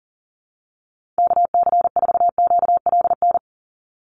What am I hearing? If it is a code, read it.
Morse code, heard as a keyed tone, about 40 wpm, 700 Hz. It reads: KC4QLN
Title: USA Call Signs 40wpm